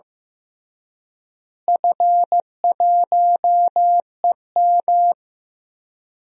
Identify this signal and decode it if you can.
Morse code, heard as a keyed tone, about 15 wpm, 700 Hz. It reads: F1EM
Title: DXCC-100 Call Signs - 4 Characters - Repeated 3x - 15wpm